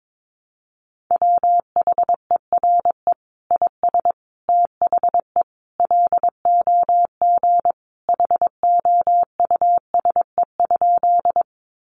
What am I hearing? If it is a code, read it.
Morse code, heard as a keyed tone, about 22 wpm, 700 Hz. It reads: WHERE IS THE LOG HOUSE?
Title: Sentences from Top 1400 Words 22wpm